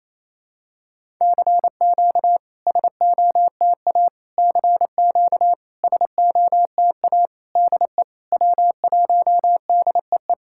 Morse code, heard as a keyed tone, about 28 wpm, 700 Hz. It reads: CQ SOTA CQ SOTA DE W1DEE
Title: SOTA QSO Practice 28wpm